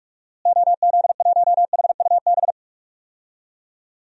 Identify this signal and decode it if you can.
Morse code, heard as a keyed tone, about 42 wpm, 700 Hz. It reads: OZ1HUB
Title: DXCC-100 Call Signs - 6 Characters - Repeated 2x - 45wpm